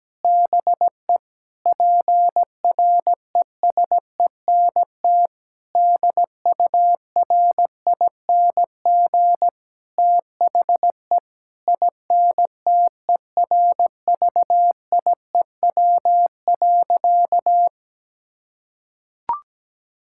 Morse code, heard as a keyed tone, about 17 wpm, 700 Hz. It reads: BE PRESENT DURING THE INTERVIEW.
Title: Sentences from Top 1600 Words 17wpm